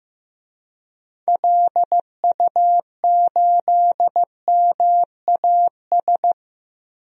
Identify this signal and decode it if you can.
Morse code, heard as a keyed tone, about 15 wpm, 700 Hz. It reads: LU8MAS
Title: DXCC-100 Call Signs - 6 Characters - Repeated 2x - 15wpm